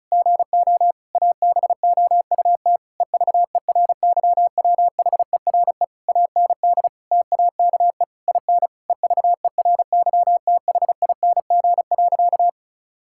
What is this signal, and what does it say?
Morse code, heard as a keyed tone, about 36 wpm, 700 Hz. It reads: GO ABOUT EVERYWHERE AND TAKE IN EVERYTHING.
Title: Sentences from Top 1600 Words 35wpm